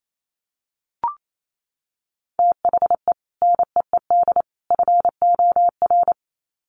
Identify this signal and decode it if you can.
Morse code, heard as a keyed tone, about 28 wpm, 700 Hz. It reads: THE NEED FOR
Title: Sets of 3 Words 28wpm